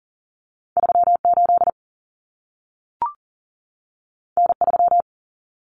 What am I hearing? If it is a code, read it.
Morse code, heard as a keyed tone, about 40 wpm, 700 Hz. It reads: 38 N3
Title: Common Cut Numbers - 2-Digit 40wpm